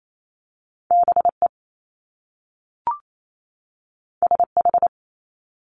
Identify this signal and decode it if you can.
Morse code, heard as a keyed tone, about 28 wpm, 700 Hz. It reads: BE SH
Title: Top Two Letter Combinations in Words 28wpm